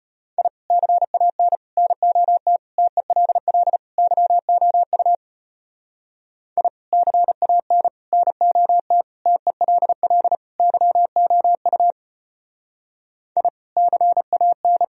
Morse code, heard as a keyed tone, about 31 wpm, 700 Hz. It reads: I CAN NOT TELL YOU I CAN NOT TELL YOU I CAN
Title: Speed-Racing - Sets of 5 Words 25wpm